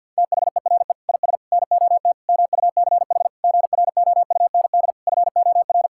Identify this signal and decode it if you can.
Morse code, heard as a keyed tone, about 45 wpm, 700 Hz. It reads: THERE IS NOT MUCH GROUND FOR
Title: Sentences from Top 1600 Words 50wpm